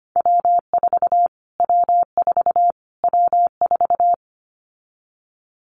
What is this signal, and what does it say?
Morse code, heard as a keyed tone, about 25 wpm, 700 Hz. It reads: W4 W4 W4
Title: Callsign Prefixes for Top 25 DXCC - Repeated 3x - 25wpm